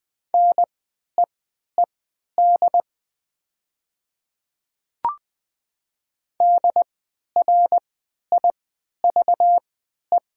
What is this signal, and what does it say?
Morse code, heard as a keyed tone, about 20 wpm, spaced as for 11 wpm, 700 Hz. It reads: NEED DRIVE
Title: 3x Character Spacing - Top 500 Words - 20wpm